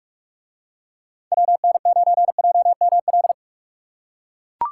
Morse code, heard as a keyed tone, about 42 wpm, 700 Hz. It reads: WN9JML